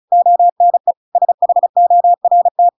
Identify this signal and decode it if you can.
Morse code, heard as a keyed tone, about 36 wpm, 700 Hz. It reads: ONE SHORT